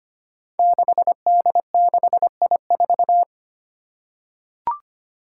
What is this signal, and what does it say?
Morse code, heard as a keyed tone, about 25 wpm, 700 Hz. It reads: 6D6I4
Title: Five Challenging Letter-Number Combinations 25wpm